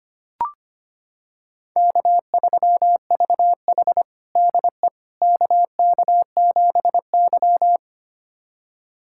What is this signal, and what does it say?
Morse code, heard as a keyed tone, about 25 wpm, 700 Hz. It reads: K3VH DE KK7Y
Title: USA Callsign DE Callsign - 4 Characters - 25wpm